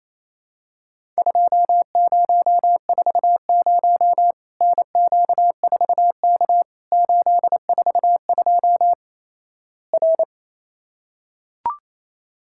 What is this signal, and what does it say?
Morse code, heard as a keyed tone, about 28 wpm, 700 Hz. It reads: 2040 NQ4K 842 R
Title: Worked All Europe DX Contest - QTC Exchange - 28wpm